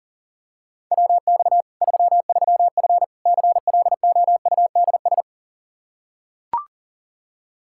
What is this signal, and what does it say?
Morse code, heard as a keyed tone, about 40 wpm, 700 Hz. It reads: WX 33F CLOUDS
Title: Practice QSO - Exchange 2 - Info - 40wpm